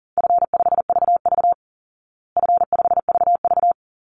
Morse code, heard as a keyed tone, about 40 wpm, 700 Hz. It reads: F5VV F5VV
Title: Easy Call Signs Repeated 3x - 40wpm